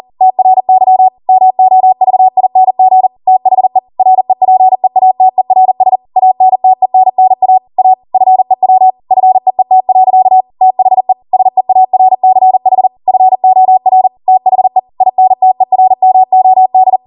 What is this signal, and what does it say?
Morse code, heard as a keyed tone, about 40 wpm, 750 Hz. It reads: TRY MOVING THE REPEATERS ANTENNA A FEW FEET. THE SEARCH FOR THE INTERMOD
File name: Morse / 40WPM